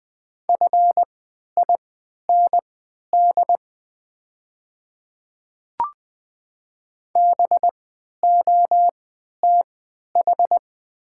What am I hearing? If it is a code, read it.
Morse code, heard as a keyed tone, about 20 wpm, spaced as for 11 wpm, 700 Hz. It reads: FIND BOTH
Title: 3x Character Spacing - Top 500 Words - 20wpm